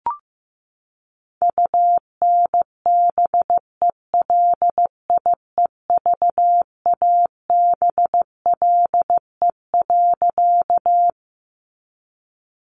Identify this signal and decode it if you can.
Morse code, heard as a keyed tone, about 16 wpm, 700 Hz. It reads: UNBELIEVABLE.
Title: A Sound Of Thunder 15wpm